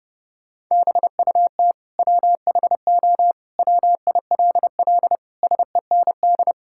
Morse code, heard as a keyed tone, about 30 wpm, 700 Hz. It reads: BUT WHO WILL SEND
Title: Sentences from Top 1600 Words 30wpm